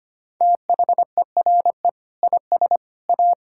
Morse code, heard as a keyed tone, about 25 wpm, 700 Hz. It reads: THERE IS A
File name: practice / 25wpm / Sentences with 2-Word Phrases - 25wpm